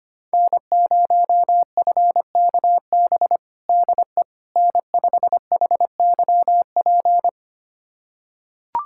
Morse code, heard as a keyed tone, about 25 wpm, 700 Hz. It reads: N0FKB DE N5HYP
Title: USA Callsign DE Callsign - 5 Characters - 25wpm